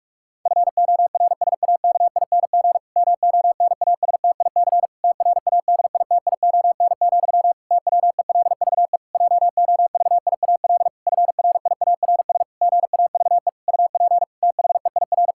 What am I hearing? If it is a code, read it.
Morse code, heard as a keyed tone, about 41 wpm, 700 Hz. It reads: FORSAKING MONASTIC TRADITION, TWELVE JOVIAL FRIARS GAVE UP THEIR
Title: Pangram Challenge v2 - 45wpm